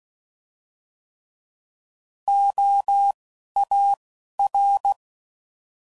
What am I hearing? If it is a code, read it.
Morse code, heard as a keyed tone, about 16 wpm, spaced as for 11 wpm, 800 Hz. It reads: OAR
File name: Prefixes / By First Letter / Os